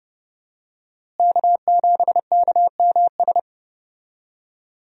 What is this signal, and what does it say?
Morse code, heard as a keyed tone, about 30 wpm, 700 Hz. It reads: K7KMS
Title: USA Call Signs - 5 Characters - Repeated 3x - 30wpm